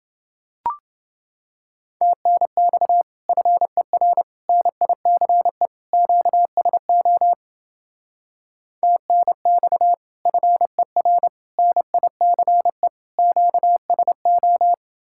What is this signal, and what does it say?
Morse code, heard as a keyed tone, about 30 wpm, 700 Hz. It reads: TNX FER NICE QSO TNX FER NICE QSO